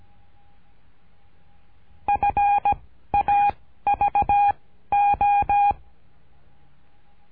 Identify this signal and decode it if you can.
Morse code, heard as a keyed tone, about 17 wpm, 810 Hz. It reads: F A V O
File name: cursos / CURSO DE TELEGRAFIA / Telegrafia - Trilha 13